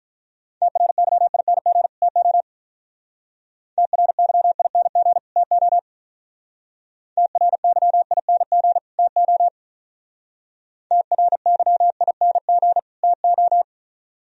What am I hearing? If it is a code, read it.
Morse code, heard as a keyed tone, about 43 wpm, 690 Hz. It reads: TRYING TO TRYING TO TRYING TO TRYING TO
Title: Speed-Racing - Sets of 2 Words 35wpm